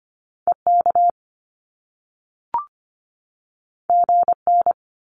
Morse code, heard as a keyed tone, about 25 wpm, 700 Hz. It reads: EK GN
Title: Top Two Letter Combinations in Words 25wpm